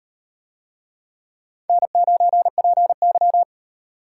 Morse code, heard as a keyed tone, about 38 wpm, 700 Hz. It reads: N9PY